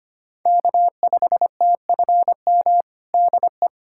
Morse code, heard as a keyed tone, about 25 wpm, 700 Hz. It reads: K5TFM DE
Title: USA Callsign DE Callsign 25wpm